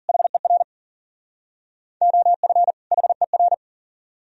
Morse code, heard as a keyed tone, about 51 wpm, 700 Hz. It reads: HER OF HER